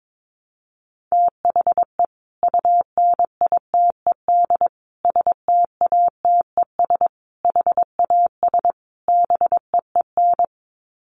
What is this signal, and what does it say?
Morse code, heard as a keyed tone, about 22 wpm, 700 Hz. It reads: THE UNITED STATES HAS BEEN
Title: Sets of 5 Words Encore2 22wpm